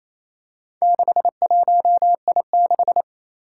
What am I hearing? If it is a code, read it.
Morse code, heard as a keyed tone, about 28 wpm, 700 Hz. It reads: 61I6
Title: Four Challenging Letter-Number Combinations 28wpm